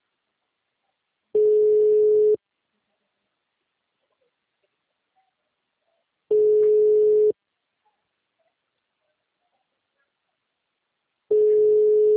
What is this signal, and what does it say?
Ringback tone (Europe, 425 Hz).